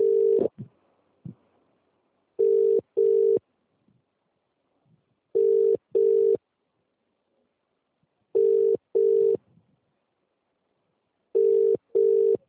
Ringback tone (UK), 400 + 450 Hz.